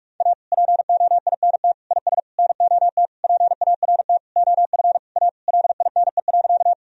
Morse code, heard as a keyed tone, about 41 wpm, 700 Hz. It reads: A POINT IS NOT PART OF A LINE.
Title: Sentences from Top 1000 Words - Review Part 1 - 45wpm